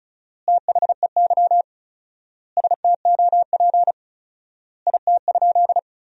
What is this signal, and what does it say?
Morse code, heard as a keyed tone, about 36 wpm, 700 Hz. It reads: THEY STOP IT?
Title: 4x Word-Spacing - Sentences from Top 300 Words - 35wpm